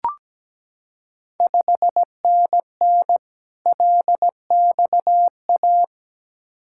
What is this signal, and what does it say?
Morse code, heard as a keyed tone, about 17 wpm, 690 Hz. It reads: 5NN LXA
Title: Ontario QSO Party Exchange 17wpm